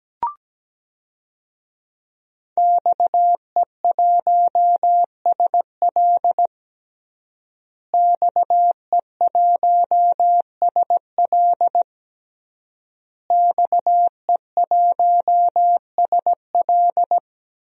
Morse code, heard as a keyed tone, about 17 wpm, 700 Hz. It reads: XE1SL XE1SL XE1SL
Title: DXCC-100 Call Signs - 5 Characters - Repeated 3x - 17wpm